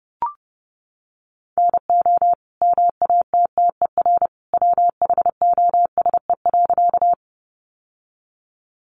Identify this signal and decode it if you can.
Morse code, heard as a keyed tone, about 30 wpm, 700 Hz. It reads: NO MATTER WHOSE.
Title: Sentences from Top 1200 Words 30wpm